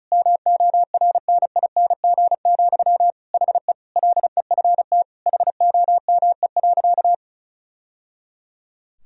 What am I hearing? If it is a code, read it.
Morse code, heard as a keyed tone, about 36 wpm, 700 Hz. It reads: MORNING, HE LEFT HOME.
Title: Sentences from Top 1000 Words - Review Part 4 - 35wpm